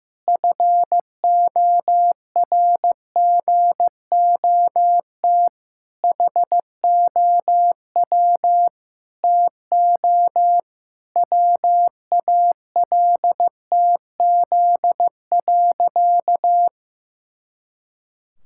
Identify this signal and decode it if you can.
Morse code, heard as a keyed tone, about 16 wpm, 700 Hz. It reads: FORGOT HOW TO WALTZ.
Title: Pangram Challenge v2 - 15wpm